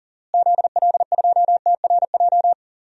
Morse code, heard as a keyed tone, about 40 wpm, 700 Hz. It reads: ZL2TRJ
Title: DXCC-100 Call Signs - 6 Characters - Repeated 2x - 40wpm